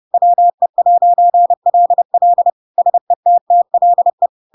Morse code, heard as a keyed tone, about 30 wpm, 700 Hz. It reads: WE'LL SETTLE
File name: Pinocchio / Pinocchio ch3 30 30 ik1hge